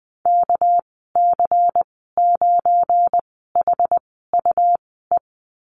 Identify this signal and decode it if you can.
Morse code, heard as a keyed tone, about 20 wpm, 700 Hz. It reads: K C 9 H U E
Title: 200 Lesson 32g - Letter k - Call Signs 20wpm